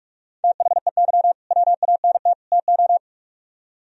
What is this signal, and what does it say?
Morse code, heard as a keyed tone, about 45 wpm, 700 Hz. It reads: THEY WANT TO